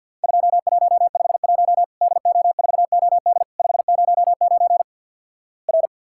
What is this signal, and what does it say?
Morse code, heard as a keyed tone, about 52 wpm, 700 Hz. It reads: 2151 DO4OD 509 R
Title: Worked All Europe DX Contest - QTC Exchange - 50wpm